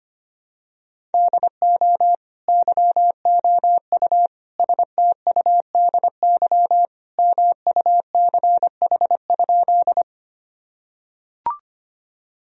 Morse code, heard as a keyed tone, about 25 wpm, 700 Hz. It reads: DO YOU STUDY MUCH?
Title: Sentences from Top 1000 Words - Review Part 1 - 25wpm